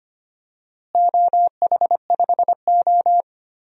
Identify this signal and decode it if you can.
Morse code, heard as a keyed tone, about 25 wpm, 700 Hz. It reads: OH5O
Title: DXCC-100 Call Signs - 4 Characters - Repeated 2x - 25wpm